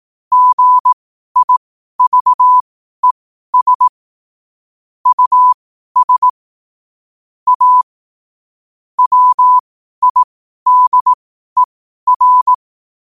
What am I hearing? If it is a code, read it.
Morse code, heard as a keyed tone, about 18 wpm, spaced as for 12 wpm, 1000 Hz. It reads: GIVES US A WIDER
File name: 12wpm